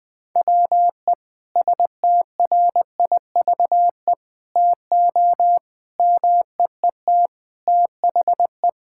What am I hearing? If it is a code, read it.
Morse code, heard as a keyed tone, about 20 wpm, 700 Hz. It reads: WE STRIVE TO MEET THE
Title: Sentences with 5-Word Phrases - 20wpm